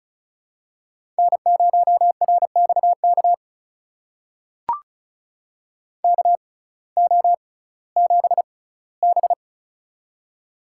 Morse code, heard as a keyed tone, about 34 wpm, spaced as for 19 wpm, 700 Hz. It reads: N0RXK KO7B
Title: USA Call Signs - 6x Character Spacing - 35wpm